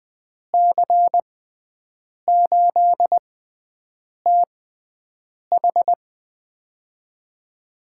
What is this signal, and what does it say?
Morse code, heard as a keyed tone, about 20 wpm, spaced as for 7 wpm, 700 Hz. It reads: C8TH
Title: USA Call Signs - 6x Character Spacing - 20wpm